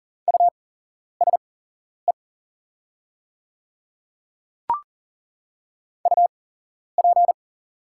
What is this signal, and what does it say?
Morse code, heard as a keyed tone, about 38 wpm, spaced as for 11 wpm, 700 Hz. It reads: USE UP
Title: 8x Character Spacing - Top 100 Words - 40wpm